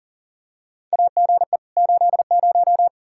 Morse code, heard as a keyed tone, about 40 wpm, 700 Hz. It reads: AGE 80
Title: Practice QSO - Exchange 3 - More Info - 40wpm